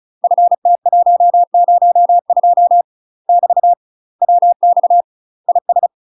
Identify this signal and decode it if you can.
Morse code, heard as a keyed tone, about 34 wpm, 700 Hz. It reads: FT102 <BT> WX IS
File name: UK MASTER QSOs TRAINING / 35WPM / UK MASTER QSOs 0004